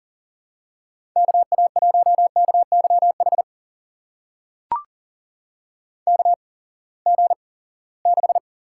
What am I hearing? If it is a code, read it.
Morse code, heard as a keyed tone, about 38 wpm, spaced as for 22 wpm, 700 Hz. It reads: KA1KYH KG6
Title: USA Call Signs - 6 Characters - 8x Character Spacing - 40wpm